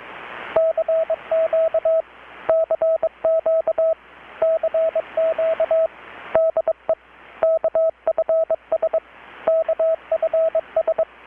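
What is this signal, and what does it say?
Morse code, heard as a keyed tone, about 21 wpm, 650 Hz, with radio noise. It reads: CQ CQ CQ DE KFS KFS
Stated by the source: Morse from KPH's Night of Nights XX maritime radiotelegraphy event, received in Chiapas, Mexico about 3,800 km away on a Kenwood R-600 with a grounded 10 m long-wire antenna.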